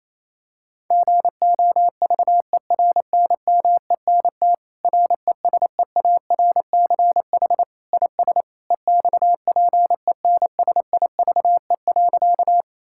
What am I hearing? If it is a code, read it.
Morse code, heard as a keyed tone, about 27 wpm, 700 Hz. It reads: GOVERNMENT RESEARCH IS EXPENSIVE.
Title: Sentences from Top 1000 Words - Review Part 6 - 28wpm